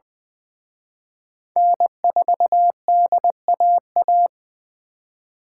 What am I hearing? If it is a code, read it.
Morse code, heard as a keyed tone, about 24 wpm, 700 Hz. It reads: N4DAA
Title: Speed Racing - USA Call Signs 20wpm